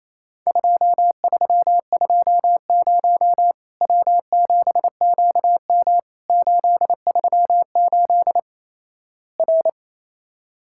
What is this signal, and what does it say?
Morse code, heard as a keyed tone, about 28 wpm, 700 Hz. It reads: 2320 W7QM 838 R
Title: Worked All Europe DX Contest - QTC Exchange - 28wpm